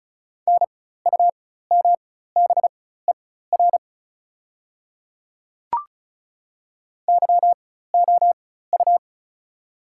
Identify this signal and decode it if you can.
Morse code, heard as a keyed tone, about 36 wpm, spaced as for 16 wpm, 690 Hz. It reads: NUMBER YOU
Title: 4x Character Spacing - Top 100 Words - 35wpm